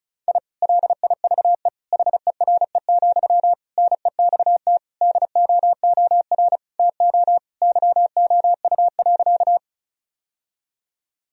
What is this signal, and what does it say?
Morse code, heard as a keyed tone, about 36 wpm, 700 Hz. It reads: I LIVE HERE, NEXT DOOR TO YOU.
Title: Sentences from Top 1000 Words - Review Part 1 - 35wpm